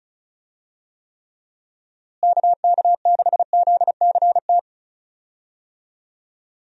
Morse code, heard as a keyed tone, about 36 wpm, 700 Hz. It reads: KK6ZCT
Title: USA Call Signs - 6 Characters - Repeated 3x - 35wpm